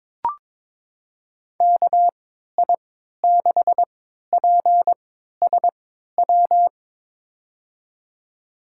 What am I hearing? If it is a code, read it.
Morse code, heard as a keyed tone, about 23 wpm, spaced as for 13 wpm, 700 Hz. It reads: KI6PSW
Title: USA Call Signs - 6 Characters - 3x Character Spacing - 22wpm